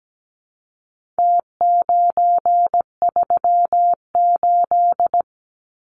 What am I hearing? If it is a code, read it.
Morse code, heard as a keyed tone, about 17 wpm, 700 Hz. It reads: T938
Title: Numbers - Contest Style - 4-Digit 17wpm